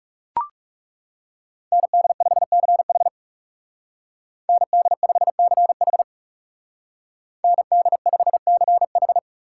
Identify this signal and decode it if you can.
Morse code, heard as a keyed tone, about 39 wpm, 700 Hz. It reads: ND5CH ND5CH ND5CH